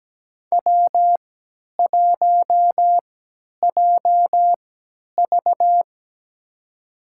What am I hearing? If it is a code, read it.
Morse code, heard as a keyed tone, about 17 wpm, spaced as for 10 wpm, 700 Hz. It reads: W1JV
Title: USA Call Signs - 4 Characters - 3x Character Spacing - 17wpm